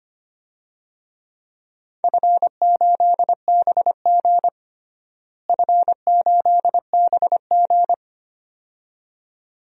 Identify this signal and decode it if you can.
Morse code, heard as a keyed tone, about 25 wpm, 700 Hz. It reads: F8BG F8BG
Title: DXCC-100 Call Signs - 4 Characters - Repeated 2x - 25wpm